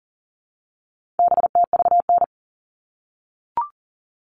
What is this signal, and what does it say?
Morse code, heard as a keyed tone, about 40 wpm, 700 Hz. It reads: BTVN
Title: All Cut Numbers - 4-Digit 40wpm